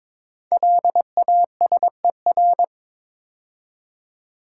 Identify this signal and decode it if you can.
Morse code, heard as a keyed tone, about 22 wpm, 700 Hz. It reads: LASER
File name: 5 Letter Words - First 3 Letters Given - 22wpm